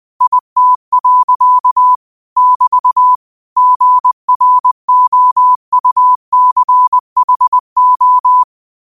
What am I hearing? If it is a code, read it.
Morse code, heard as a keyed tone, about 20 wpm, 1000 Hz. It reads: IT. <BT> GROUCHO
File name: qotdfiles / 20wpm